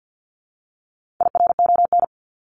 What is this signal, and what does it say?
Morse code, heard as a keyed tone, about 50 wpm, 700 Hz. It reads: IRON
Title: Top 1751-2000 Words 50wpm